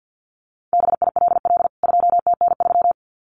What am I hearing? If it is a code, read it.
Morse code, heard as a keyed tone, about 48 wpm, 700 Hz. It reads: BILL 2TN3
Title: CWT Exchange 50wpm